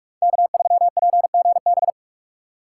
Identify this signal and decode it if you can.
Morse code, heard as a keyed tone, about 43 wpm, 700 Hz. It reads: K3PGB
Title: USA Call Signs - 2x Character Spacing - 45wpm